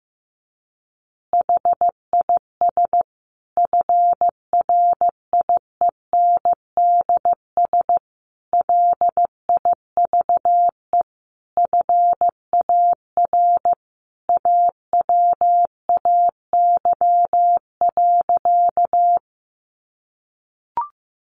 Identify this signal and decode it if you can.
Morse code, heard as a keyed tone, about 15 wpm, 700 Hz. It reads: HIS FRIENDS LIVE FAR AWAY.
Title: Additional Easy Phrases 15wpm